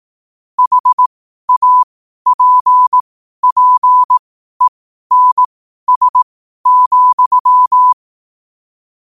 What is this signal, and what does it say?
Morse code, heard as a keyed tone, about 18 wpm, spaced as for 12 wpm, 1000 Hz. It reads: HAPPENS,
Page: New quotes every day in morse code at 12 Words per minute.